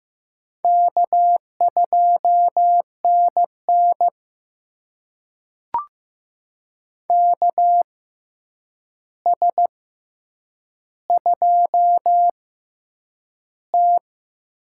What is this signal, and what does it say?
Morse code, heard as a keyed tone, about 15 wpm, spaced as for 8 wpm, 700 Hz. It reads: K2NN KS2T
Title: USA Call Signs - 4 Characters - 6x Character Spacing - 15wpm